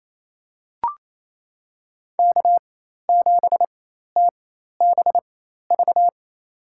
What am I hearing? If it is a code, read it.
Morse code, heard as a keyed tone, about 28 wpm, spaced as for 13 wpm, 700 Hz. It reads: K7TBV
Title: USA Call Signs - 5 Characters - 4x Character Spacing - 28wpm